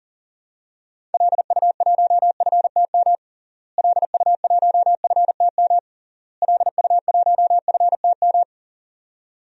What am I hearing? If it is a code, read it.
Morse code, heard as a keyed tone, about 40 wpm, 700 Hz. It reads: LU1FTM LU1FTM LU1FTM
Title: DXCC-100 Call Signs - 6 Characters - Repeated 3x - 40wpm